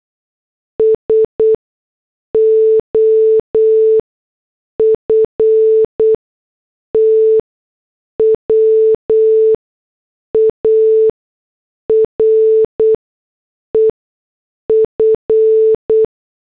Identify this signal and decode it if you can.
Morse code, heard as a keyed tone, about 8 wpm, spaced as for 6 wpm, 440 Hz. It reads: SOFTWARE F